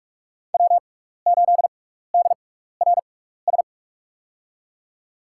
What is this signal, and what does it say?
Morse code, heard as a keyed tone, about 45 wpm, spaced as for 15 wpm, 700 Hz. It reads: W8DRS
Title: USA Call Signs - 5 Characters - 6x Character Spacing - 45wpm